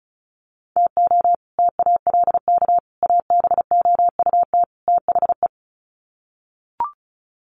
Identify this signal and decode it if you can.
Morse code, heard as a keyed tone, about 34 wpm, 700 Hz. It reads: TO TALK ABOUT THE
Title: Sets of 4 Words 35wpm